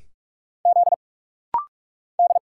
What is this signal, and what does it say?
Morse code, heard as a keyed tone, about 46 wpm, 700 Hz. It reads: Z D
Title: Single Letter-Number Alt - Rapid-Fire 45wpm